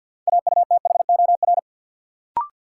Morse code, heard as a keyed tone, about 51 wpm, 700 Hz. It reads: AUTHOR
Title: Sentences from Top 1200 Words - New Word Drill - 50wpm